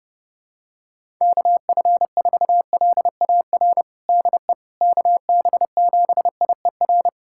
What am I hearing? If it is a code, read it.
Morse code, heard as a keyed tone, about 31 wpm, 700 Hz. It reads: KF4LAR DE KB7IER
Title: USA Callsign DE Callsign - 6 Characters - 30wpm